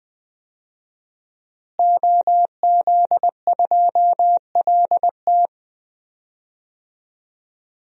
Morse code, heard as a keyed tone, about 20 wpm, 700 Hz. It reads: OZ2LT
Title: DXCC-100 Call Signs - 5 Characters - Repeated 2x - 20wpm